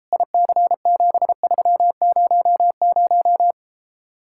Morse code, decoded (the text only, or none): IC7300